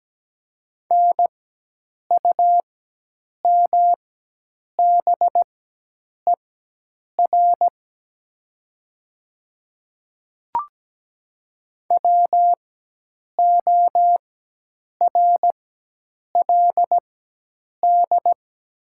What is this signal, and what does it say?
Morse code, heard as a keyed tone, about 17 wpm, spaced as for 8 wpm, 700 Hz. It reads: NUMBER WORLD
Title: 4x Character Spacing - Top 200 Words - 17wpm